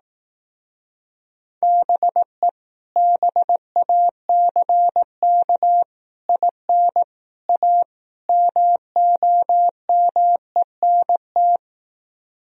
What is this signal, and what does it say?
Morse code, heard as a keyed tone, about 18 wpm, 690 Hz. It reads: BE BACK IN A MOMENT
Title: Speed-Racing - Sets of 5 Words 15wpm